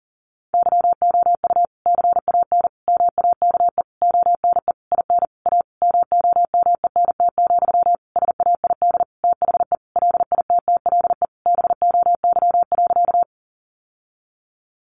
Morse code, heard as a keyed tone, about 40 wpm, 700 Hz. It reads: YOU CAN MAKE ONE IN A MOMENT, SAID THE LITTLE BOY.
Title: Sentences from Top 1000 Words 40wpm